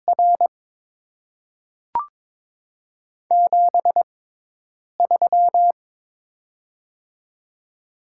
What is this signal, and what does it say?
Morse code, heard as a keyed tone, about 22 wpm, spaced as for 8 wpm, 700 Hz. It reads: R 73
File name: Top 90 QSO Elements - 6x Character Spacing - 22wpm